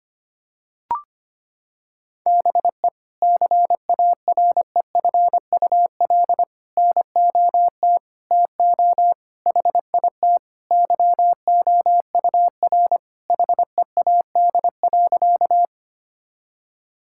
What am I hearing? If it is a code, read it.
Morse code, heard as a keyed tone, about 25 wpm, 700 Hz. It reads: BE CAREFUL NOT TO HIT YOUR HEAD.
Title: Sentences from Top 1600 Words 25wpm